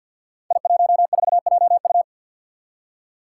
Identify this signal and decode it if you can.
Morse code, heard as a keyed tone, about 51 wpm, 700 Hz. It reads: I14JU
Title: Five Challenging Letter-Number Combinations 50wpm